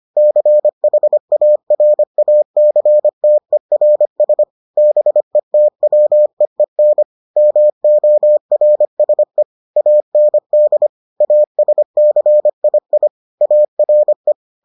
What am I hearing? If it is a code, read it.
Morse code, heard as a keyed tone, about 25 wpm, 600 Hz. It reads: CHARACTERS BETWEEN MORSE AND ASCII ARE